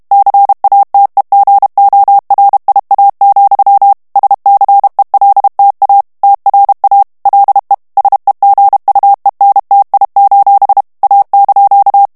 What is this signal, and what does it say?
Morse code, heard as a keyed tone, about 32 wpm, 800 Hz. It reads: CATEGORIA, SCELTA TRA LE SEGUENTI: A)
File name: texte clair 2 vitesse 30 mots minute